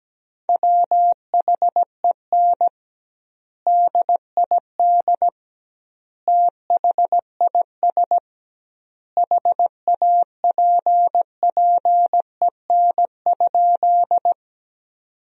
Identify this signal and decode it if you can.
Morse code, heard as a keyed tone, about 17 wpm, 700 Hz. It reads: WHEN DID THIS HAPPEN?
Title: 2x Word-Spacing - Sentences from Top 400 Words - 17wpm